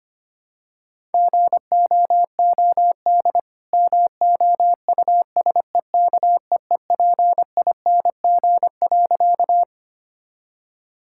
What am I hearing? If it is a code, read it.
Morse code, heard as a keyed tone, about 25 wpm, 700 Hz. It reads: GOOD MOUSEKEEPING.
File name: practice / 25wpm / All-Age Appropriate Jokes - 25wpm